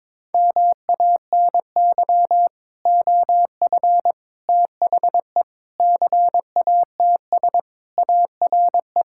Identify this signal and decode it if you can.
Morse code, heard as a keyed tone, about 22 wpm, 700 Hz. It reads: MANY OF THE CATS ARE
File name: practice / 22wpm / Sentences with 3-Word Phrases - 22wpm